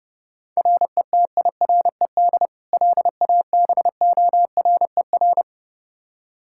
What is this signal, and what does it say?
Morse code, heard as a keyed tone, about 32 wpm, 700 Hz. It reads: RETIRED LABORER
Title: Practice QSO - Exchange 3 - More Info - 30wpm